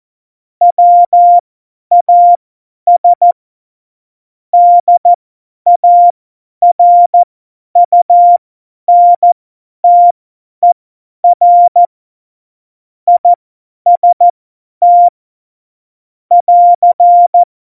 Morse code, heard as a keyed tone, about 14 wpm, spaced as for 10 wpm, 700 Hz. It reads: WAS DARUNTER IST <AR>
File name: Morsekurs / Witze / Witz 01. Dezember 2020, 50 F70